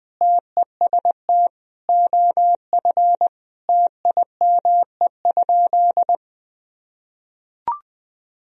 Morse code, heard as a keyed tone, about 20 wpm, 700 Hz. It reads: TEST OF TIME?
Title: Sentences from Top 1000 Words - Review Part 3 - 20wpm